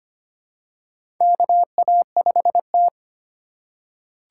Morse code, heard as a keyed tone, about 25 wpm, 700 Hz. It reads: KA5T
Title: USA Call Signs - 4 Characters - 4x Character Spacing - 25wpm